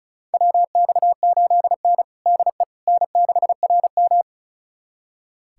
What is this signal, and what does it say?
Morse code, heard as a keyed tone, about 34 wpm, 700 Hz. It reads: WX8N DE N6RM
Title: USA Callsign DE Callsign - 4 Characters - 35wpm